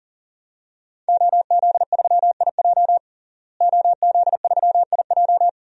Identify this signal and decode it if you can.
Morse code, heard as a keyed tone, about 40 wpm, 700 Hz. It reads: OZ3IJ OZ3IJ
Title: DXCC-100 Call Signs - 5 Characters - Repeated 3x - 40wpm